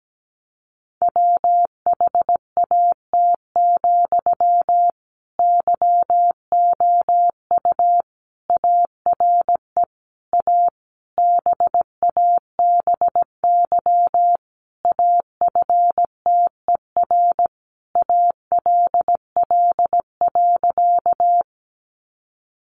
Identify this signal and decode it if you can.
Morse code, heard as a keyed tone, about 17 wpm, 700 Hz. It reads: WHAT, YOU ARE A BABY AFTER ALL.
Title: Sentences from Top 1300 Words 17wpm